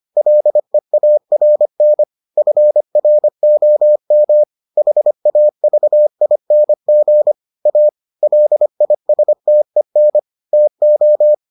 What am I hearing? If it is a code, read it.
Morse code, heard as a keyed tone, about 25 wpm, 600 Hz. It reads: LEARN FROM HAVING A LISTEN TO